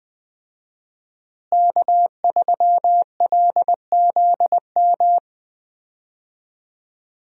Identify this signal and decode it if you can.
Morse code, heard as a keyed tone, about 21 wpm, 700 Hz. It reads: K3LZM
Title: USA Call Signs - 5 Characters - Repeated 3x - 20wpm